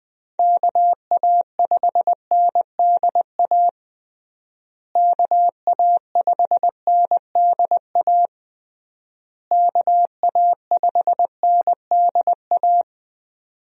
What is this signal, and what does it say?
Morse code, heard as a keyed tone, about 20 wpm, 700 Hz. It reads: KA5NDA KA5NDA KA5NDA
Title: USA Call Signs - 6 Characters - Repeated 3x - 20wpm